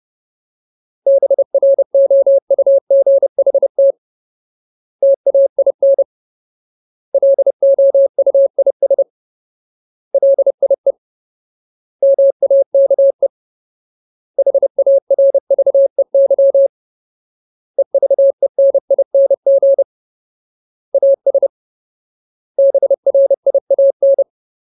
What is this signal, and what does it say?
Morse code, heard as a keyed tone, about 30 wpm, 550 Hz. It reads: BROUGHT TAIN LOUIS LIE MAKE HARVEY EVENING AS BRIAN
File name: Ultimate Word List / 30WPM Ultimate